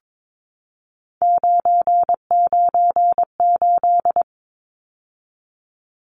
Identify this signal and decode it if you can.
Morse code, heard as a keyed tone, about 22 wpm, 700 Hz. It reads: 998
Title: Numbers - 3-Digit 22wpm